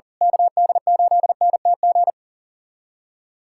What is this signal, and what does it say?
Morse code, heard as a keyed tone, about 40 wpm, 700 Hz. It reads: KD8NTG
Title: USA Callsign DE Callsign - 6 Characters - 40wpm